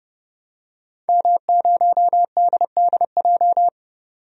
Morse code, heard as a keyed tone, about 32 wpm, 700 Hz. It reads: M0DDJ
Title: DXCC-100 Call Signs - 5 Characters - 30wpm